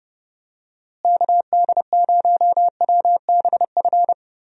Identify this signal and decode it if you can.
Morse code, heard as a keyed tone, about 30 wpm, 700 Hz. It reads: KD0WBF
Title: USA Call Signs - 6 Characters - Repeated 3x - 30wpm